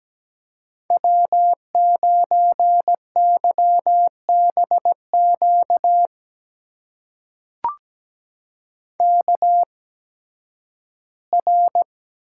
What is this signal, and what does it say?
Morse code, heard as a keyed tone, about 17 wpm, spaced as for 11 wpm, 700 Hz. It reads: W9YBQ KR
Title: USA Call Signs - 5 Characters - 8x Character Spacing - 17wpm